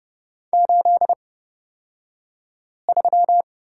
Morse code, heard as a keyed tone, about 30 wpm, spaced as for 5 wpm, 700 Hz. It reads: 83
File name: Lesson 37d - Letter x - Two Characters 30wpm